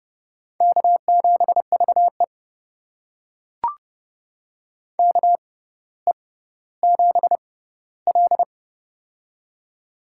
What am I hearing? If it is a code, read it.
Morse code, heard as a keyed tone, about 30 wpm, spaced as for 15 wpm, 700 Hz. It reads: K7VE KE7L
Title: USA Call Signs - 4 Characters - 6x Character Spacing - 30wpm